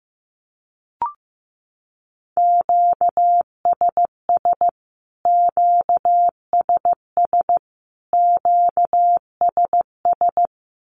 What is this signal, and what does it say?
Morse code, heard as a keyed tone, about 14 wpm, 700 Hz. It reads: QSS QSS QSS
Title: Q-Codes Repeated 3x 15wpm